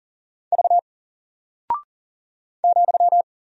Morse code, heard as a keyed tone, about 40 wpm, 700 Hz. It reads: V ,
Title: Single Character (Letter:Number:Punctuation:Prosign) 40wpm